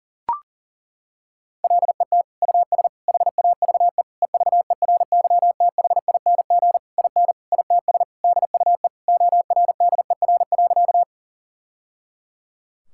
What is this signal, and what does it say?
Morse code, heard as a keyed tone, about 40 wpm, 700 Hz. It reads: LET US HAVE EVERYTHING IN ITS DUE ORDER.
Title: Sentences from Top 1000 Words - Review Part 6 - 40wpm